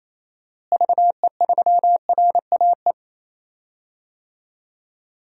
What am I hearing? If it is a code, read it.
Morse code, heard as a keyed tone, about 27 wpm, 700 Hz. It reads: VE3RAE
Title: DXCC-100 Call Signs - 6 Characters - Repeated 2x - 28wpm